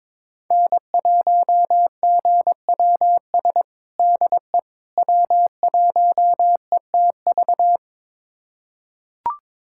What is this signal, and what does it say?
Morse code, heard as a keyed tone, about 22 wpm, 700 Hz. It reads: N1GWS DE W1ETV
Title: USA Callsign DE Callsign - 5 Characters - 22wpm